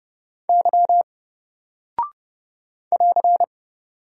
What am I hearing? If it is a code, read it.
Morse code, heard as a keyed tone, about 30 wpm, 700 Hz. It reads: Y <AR>
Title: Single Character (Letter:Number:Punctuation:Prosign) 30wpm